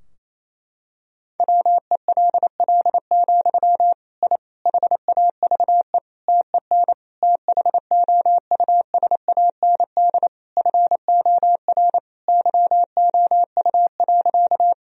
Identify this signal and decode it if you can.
Morse code, heard as a keyed tone, about 28 wpm, 700 Hz. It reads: WELL, I HAVE TEN THOUSAND FOR YOU.
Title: Sentences from Top 1000 Words - Review Part 3 - 28wpm